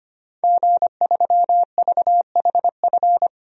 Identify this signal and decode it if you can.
Morse code, heard as a keyed tone, about 25 wpm, 700 Hz. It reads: G3VHF
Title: DXCC-100 Call Signs - 5 Characters - 25wpm